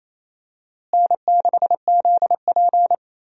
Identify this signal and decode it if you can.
Morse code, heard as a keyed tone, about 28 wpm, 700 Hz. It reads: N6ZP
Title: USA Call Signs - 4 Characters - Repeated 3x - 28wpm